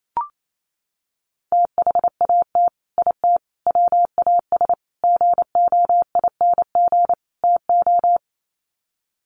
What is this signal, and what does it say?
Morse code, heard as a keyed tone, about 28 wpm, 700 Hz. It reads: THAT IT WAS GOING TO
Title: Sets of 5 Words Encore2 28wpm